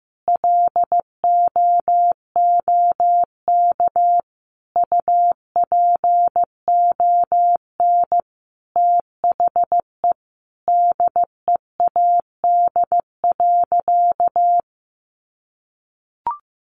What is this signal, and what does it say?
Morse code, heard as a keyed tone, about 15 wpm, 700 Hz. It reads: LOOK UPON THE DEAD.
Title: Sentences from Top 1500 Words 15wpm